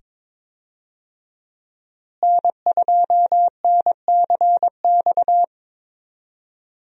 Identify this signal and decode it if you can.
Morse code, heard as a keyed tone, about 22 wpm, 700 Hz. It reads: N2NCX
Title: USA Call Signs - 5 Characters - Repeated 2x - 22wpm